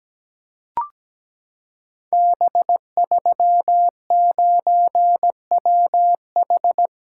Morse code, heard as a keyed tone, about 17 wpm, 700 Hz. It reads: B39WH
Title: Code Groups - 5 Letter:Numbers - 17wpm